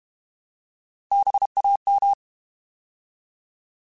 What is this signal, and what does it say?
Morse code, heard as a keyed tone, about 32 wpm, 800 Hz. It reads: BAM
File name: Alphabetical / Alphabetical mp3s / Bs @30